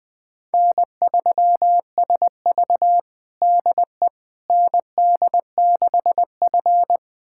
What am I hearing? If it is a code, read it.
Morse code, heard as a keyed tone, about 20 wpm, 700 Hz. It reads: N3SV DE ND6F
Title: USA Callsign DE Callsign - 4 Characters - 20wpm